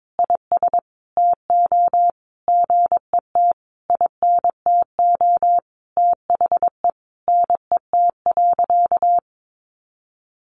Morse code, heard as a keyed tone, about 22 wpm, 700 Hz. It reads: IS TO GET INTO THE NET.
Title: Sentences from Top 1400 Words 22wpm